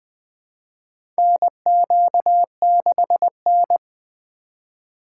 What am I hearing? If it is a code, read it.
Morse code, heard as a keyed tone, about 20 wpm, 700 Hz. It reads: NQ6N
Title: USA Call Signs - 3x Character Spacing - 20wpm